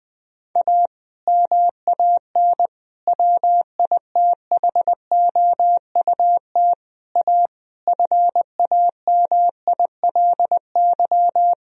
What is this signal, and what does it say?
Morse code, heard as a keyed tone, about 20 wpm, 700 Hz. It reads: A MAN WITHOUT A FAMILY
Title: Sentences from Top 1000 Words - Review Part 3 - 20wpm